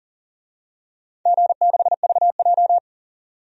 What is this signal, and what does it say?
Morse code, heard as a keyed tone, about 40 wpm, 700 Hz. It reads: G6VJ
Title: DXCC-100 Call Signs - 4 Characters - 40wpm